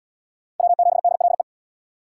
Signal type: Morse code

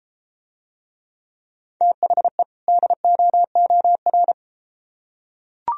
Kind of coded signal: Morse code

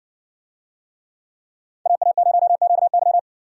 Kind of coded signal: Morse code

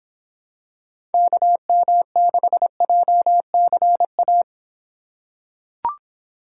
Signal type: Morse code

